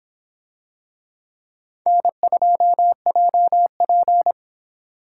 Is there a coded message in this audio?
Yes: Morse code from 1.9 to 4.3 s, heard as a keyed tone, about 26 wpm, 700 Hz. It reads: N2JP